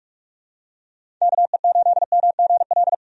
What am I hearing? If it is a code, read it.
Morse code, heard as a keyed tone, about 46 wpm, 700 Hz. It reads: KE8MGL
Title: USA Call Signs - 6 Characters - 45wpm